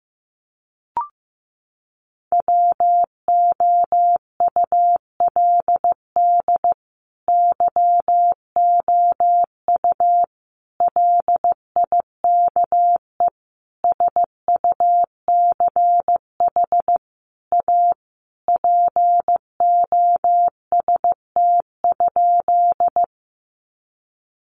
Morse code, heard as a keyed tone, about 15 wpm, 700 Hz. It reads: WOULD YOU LIKE SUCH A POST?
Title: Sentences from Top 1000 Words 15wpm